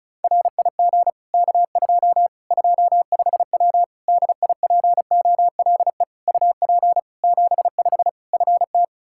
Morse code, heard as a keyed tone, about 36 wpm, 700 Hz. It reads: RIG K2 25W DIPOLE UP 75 FT
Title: Practice QSO - Exchange 2 - Info - 35wpm